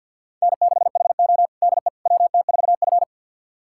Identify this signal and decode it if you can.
Morse code, heard as a keyed tone, about 47 wpm, 700 Hz. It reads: N6HO DE WT4F
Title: USA Callsign DE Callsign - 4 Characters - 50wpm